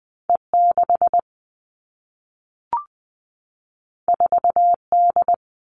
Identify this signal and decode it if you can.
Morse code, heard as a keyed tone, about 20 wpm, 700 Hz. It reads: E6 4D
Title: All Cut Numbers - 2-Digit 20wpm